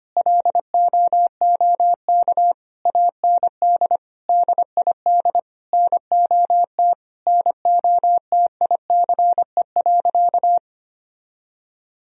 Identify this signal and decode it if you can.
Morse code, heard as a keyed tone, about 25 wpm, 700 Hz. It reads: LOOK AND DID NOT NOTICE.
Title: Sentences from Top 1000 Words - Review Part 2 - 25wpm